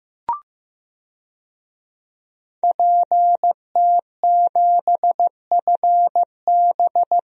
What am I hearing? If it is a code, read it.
Morse code, heard as a keyed tone, about 15 wpm, 700 Hz. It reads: PT7FB
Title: DXCC-100 Call Signs - 5 Characters - Repeated 2x - 15wpm